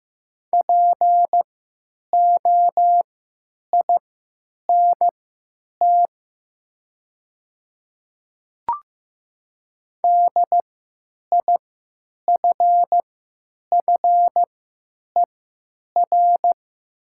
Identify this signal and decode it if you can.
Morse code, heard as a keyed tone, about 15 wpm, spaced as for 9 wpm, 700 Hz. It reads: POINT DIFFER
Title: 3x Character Spacing - Top 200 Words - 15wpm